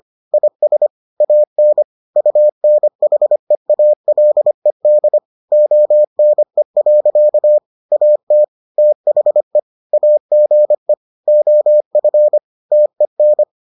Morse code, heard as a keyed tone, about 25 wpm, 600 Hz. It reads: IS AN UNHEALED ONE. AT THE AGE OF TEN